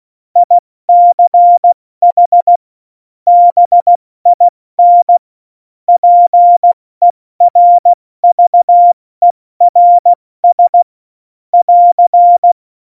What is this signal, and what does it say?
Morse code, heard as a keyed tone, about 16 wpm, 700 Hz. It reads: ICH BIN PERVERS <AR>
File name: Morsekurs / Witze / Witz 05 Mai, 2020 70 F80